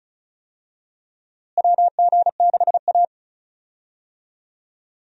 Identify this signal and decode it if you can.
Morse code, heard as a keyed tone, about 34 wpm, 700 Hz. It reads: WG6A
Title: USA Call Signs - 4 Characters - Repeated 2x - 35wpm